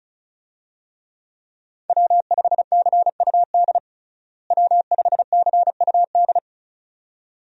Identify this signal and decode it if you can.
Morse code, heard as a keyed tone, about 36 wpm, 700 Hz. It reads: W5CUD W5CUD
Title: USA Call Signs - 5 Characters - Repeated 2x - 35wpm